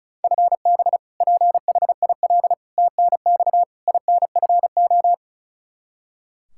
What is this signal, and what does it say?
Morse code, heard as a keyed tone, about 36 wpm, 700 Hz. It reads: FB PHIL TNX INFO
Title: Practice QSO - Exchange 3 - More Info - 35wpm